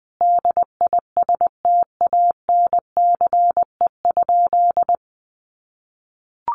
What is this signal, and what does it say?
Morse code, heard as a keyed tone, about 20 wpm, 700 Hz. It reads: DISTANCE?
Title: Sentences from Top 1200 Words 20wpm